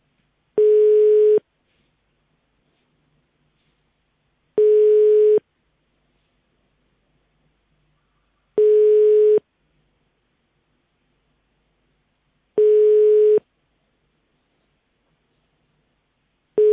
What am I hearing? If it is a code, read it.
Telephone call progress tone (425 Hz, 1 s on, 3 s off), heard as a telephone network's tone, about 420 Hz. Ringback tone.